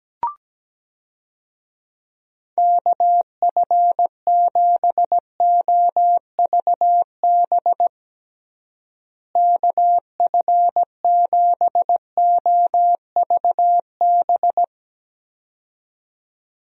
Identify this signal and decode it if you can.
Morse code, heard as a keyed tone, about 17 wpm, 700 Hz. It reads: KF7OVB KF7OVB
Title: USA Call Signs - 6 Characters - Repeated 2x - 17wpm